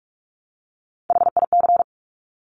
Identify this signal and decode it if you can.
Morse code, heard as a keyed tone, about 46 wpm, 690 Hz. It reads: HIC
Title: Top Three Letter Combinations in Words 45wpm